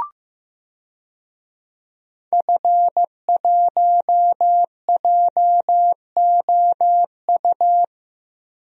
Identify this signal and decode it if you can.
Morse code, heard as a keyed tone, about 15 wpm, 700 Hz. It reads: F1JOU
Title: DXCC-100 Call Signs - 5 Characters - 15wpm